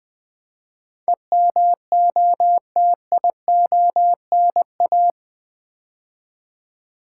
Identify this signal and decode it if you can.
Morse code, heard as a keyed tone, about 20 wpm, 700 Hz. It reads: EMOTIONA
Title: Build A Word - 4 Letters At A Time - 11-Letter Words - 20wpm